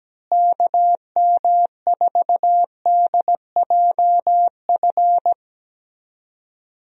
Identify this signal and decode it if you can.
Morse code, heard as a keyed tone, about 17 wpm, 700 Hz. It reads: KM4DJF
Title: USA Call Signs - 6 Characters - Repeated 2x - 17wpm